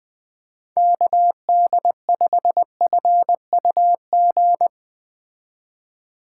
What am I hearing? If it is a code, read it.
Morse code, heard as a keyed tone, about 20 wpm, 700 Hz. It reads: KD5FUG